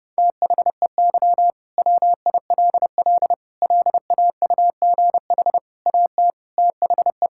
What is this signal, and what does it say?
Morse code, heard as a keyed tone, about 30 wpm, 700 Hz. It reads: THEY WILL LAUGH AT THE
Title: Sentences from Top 1000 Words - Review Part 6 - 30wpm